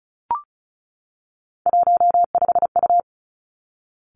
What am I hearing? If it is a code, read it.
Morse code, heard as a keyed tone, about 36 wpm, 700 Hz. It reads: 15U
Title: Three Challenging Letter-Number Combinations 35wpm